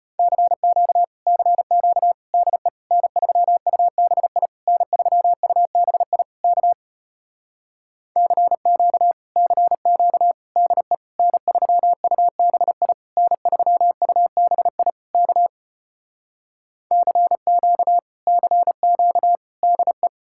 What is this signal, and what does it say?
Morse code, heard as a keyed tone, about 33 wpm, 700 Hz. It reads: CQ CQ DE N3UBI N3UBI K CQ CQ DE N3UBI N3UBI K CQ CQ DE